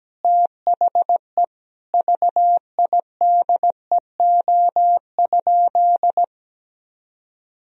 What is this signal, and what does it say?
Morse code, heard as a keyed tone, about 17 wpm, 700 Hz. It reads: THE VIDEO?
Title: Sentences from Top 1000 Words - Review Part 4 - 17wpm